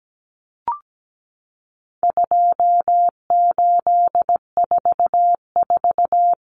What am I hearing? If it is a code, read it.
Morse code, heard as a keyed tone, about 17 wpm, 700 Hz. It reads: 2844
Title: Numbers - 4-Digit 17wpm